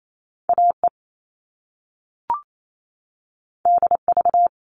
Morse code, heard as a keyed tone, about 28 wpm, 700 Hz. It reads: AE DV
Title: All Cut Numbers - 2-Digit 28wpm